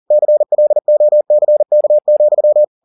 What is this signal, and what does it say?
Morse code, heard as a keyed tone, about 40 wpm, 600 Hz. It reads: CLOCK,